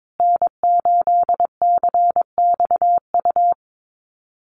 Morse code, heard as a keyed tone, about 22 wpm, 700 Hz. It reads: N8CXU
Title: USA Call Signs Repeated 2x 22wpm